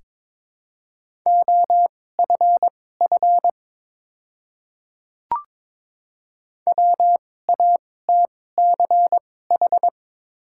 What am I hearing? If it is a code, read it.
Morse code, heard as a keyed tone, about 22 wpm, spaced as for 16 wpm, 700 Hz. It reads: OFF WATCH
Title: 2x Character Spacing - Top 300 Words - 22wpm